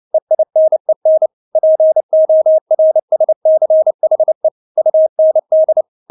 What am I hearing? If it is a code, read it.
Morse code, heard as a keyed tone, about 29 wpm, 630 Hz. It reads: EINEN PORSCHE UND
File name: Witz10 29wpm